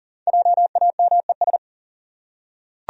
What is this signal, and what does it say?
Morse code, heard as a keyed tone, about 38 wpm, 700 Hz. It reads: JAMES
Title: USA Common Names - 40wpm